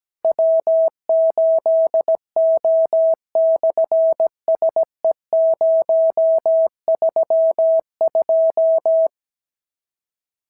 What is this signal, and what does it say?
Morse code, heard as a keyed tone, about 17 wpm, 650 Hz. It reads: W8O/SE032
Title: SOTA QSO Practice 17wpm